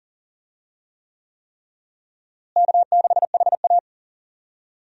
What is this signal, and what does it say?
Morse code, heard as a keyed tone, about 40 wpm, 700 Hz. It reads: K6HA
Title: USA Call Signs - 4 Characters - Repeated 3x - 40wpm